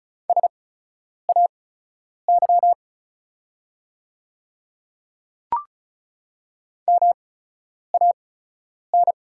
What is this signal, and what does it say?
Morse code, heard as a keyed tone, about 36 wpm, spaced as for 10 wpm, 700 Hz. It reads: SAY MAN
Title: 8x Character Spacing - Top 300 Words - 35wpm